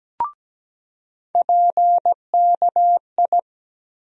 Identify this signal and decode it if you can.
Morse code, heard as a keyed tone, about 17 wpm, 700 Hz. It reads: PKI
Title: USA Call Sign Postfixes - 3 Letters 17wpm